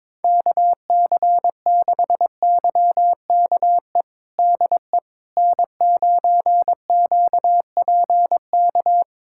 Morse code, heard as a keyed tone, about 22 wpm, 700 Hz. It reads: KC6YKE DE N9QPK
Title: USA Callsign DE Callsign 22wpm